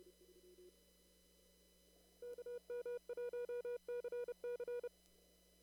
Morse code, heard as a keyed tone, about 30 wpm, 480 Hz. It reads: KM1CC
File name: CW / kt5j wpx cw-op-k5pi